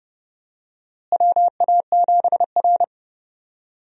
Morse code, heard as a keyed tone, about 30 wpm, 700 Hz. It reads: WA7R
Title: USA Call Signs - 4 Characters - Repeated 2x - 30wpm